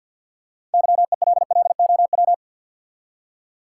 Morse code, heard as a keyed tone, about 48 wpm, 700 Hz. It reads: YELLOW
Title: Six Letter Words - 50wpm